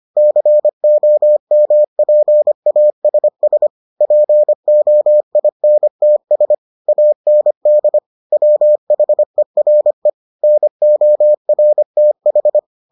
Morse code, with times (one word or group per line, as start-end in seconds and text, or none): 0.2-3.7 COMPASS
4.0-6.5 POINTS
6.9-8.0 AND
8.3-10.1 WHERE
10.4-12.6 NORTH